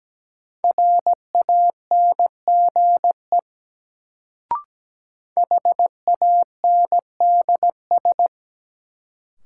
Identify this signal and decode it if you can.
Morse code, heard as a keyed tone, about 17 wpm, 700 Hz. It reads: RANGE HANDS
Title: Sets of 4 Words - New Word Drill - 17wpm